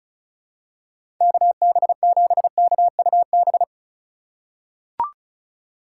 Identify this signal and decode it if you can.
Morse code, heard as a keyed tone, about 36 wpm, 700 Hz. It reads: KB7KUB
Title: USA Call Signs - 6 Characters - 3x Character Spacing - 35wpm